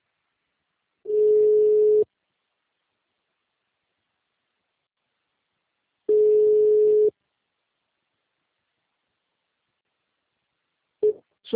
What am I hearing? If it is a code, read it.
Ringback tone (Europe, 425 Hz), with speech.
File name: out- -2004- - - .428